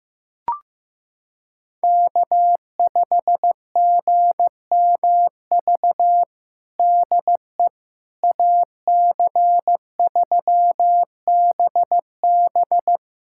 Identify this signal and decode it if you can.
Morse code, heard as a keyed tone, about 16 wpm, 700 Hz. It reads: K5GMV DE AC3BB
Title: USA Callsign DE Callsign - 5 Characters - 15wpm